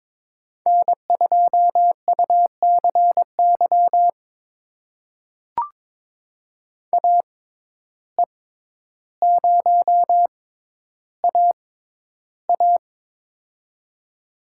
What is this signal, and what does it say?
Morse code, heard as a keyed tone, about 23 wpm, spaced as for 11 wpm, 700 Hz. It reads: N2UCY AE0AA
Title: USA Call Signs - 5 Characters - 6x Character Spacing - 22wpm